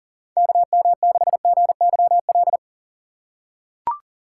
Morse code, heard as a keyed tone, about 40 wpm, 700 Hz. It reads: KM6GYL
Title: USA Call Signs - 4x Character Spacing - 40wpm